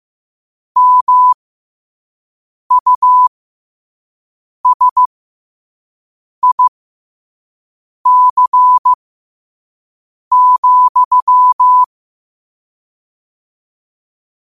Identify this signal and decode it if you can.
Morse code, heard as a keyed tone, about 15 wpm, spaced as for 5 wpm, 1000 Hz. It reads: MUSIC,
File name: qotdfiles / 5wpm